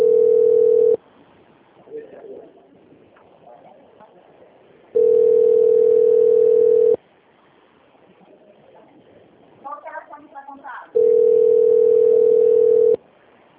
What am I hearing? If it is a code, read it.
Ringback tone (North America), 440 + 480 Hz, with speech.
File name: out- - - - - .17